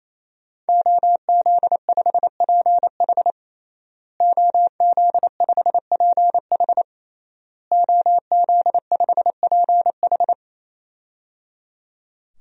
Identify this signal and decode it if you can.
Morse code, heard as a keyed tone, about 28 wpm, 700 Hz. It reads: OZ5PH OZ5PH OZ5PH
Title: DXCC-100 Call Signs - 5 Characters - Repeated 3x - 28wpm